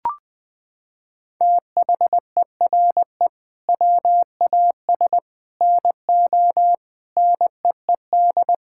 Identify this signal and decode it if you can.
Morse code, heard as a keyed tone, about 20 wpm, 700 Hz. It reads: THERE WAS NO NEED
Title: Sentences from Top 1600 Words 20wpm